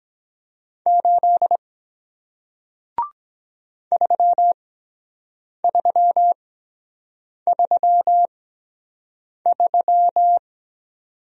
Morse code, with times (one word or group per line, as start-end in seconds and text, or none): 0.9-1.6 8
3.9-10.4 3333